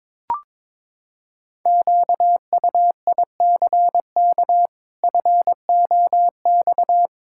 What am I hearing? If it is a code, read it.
Morse code, heard as a keyed tone, about 22 wpm, 700 Hz. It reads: QUICK FOX
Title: Pangram Challenge v2 - 22wpm